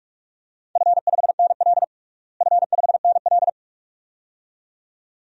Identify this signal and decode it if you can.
Morse code, heard as a keyed tone, about 46 wpm, 700 Hz. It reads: F5NL F5NL
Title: DXCC-100 Call Signs - 4 Characters - Repeated 2x - 45wpm